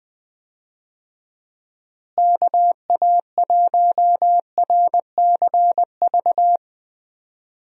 Morse code, heard as a keyed tone, about 20 wpm, 700 Hz. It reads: KA1RCV
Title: USA Call Signs - 6 Characters - Repeated 2x - 20wpm